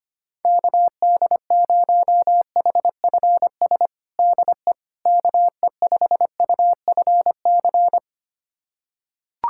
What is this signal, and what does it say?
Morse code, heard as a keyed tone, about 25 wpm, 700 Hz. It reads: KD0HFS DE KE5UFC
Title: USA Callsign DE Callsign - 6 Characters - 25wpm